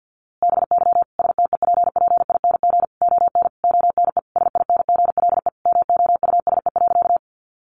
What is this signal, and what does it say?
Morse code, heard as a keyed tone, about 48 wpm, 700 Hz. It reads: BY STEPPING ON ONE SINGLE MOUSE.
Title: A Sound Of Thunder 50wpm